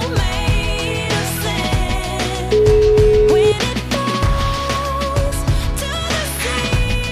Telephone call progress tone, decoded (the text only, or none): ringback tone